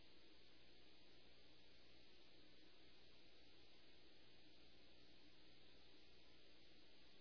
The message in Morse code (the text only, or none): none